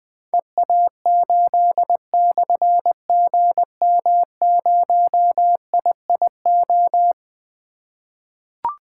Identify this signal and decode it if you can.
Morse code, heard as a keyed tone, about 20 wpm, 700 Hz. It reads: EA8/GM0IIO
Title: Difficult Call Signs 20wpm